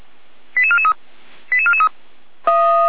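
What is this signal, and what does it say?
Five-tone selective call (ZVEI), heard as quick runs of five tones, tones of about 70 ms. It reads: 88443 88443
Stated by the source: Five-tone ZVEI selective call received by SDR, logged on the Feuerwehr Orlamünde audio server.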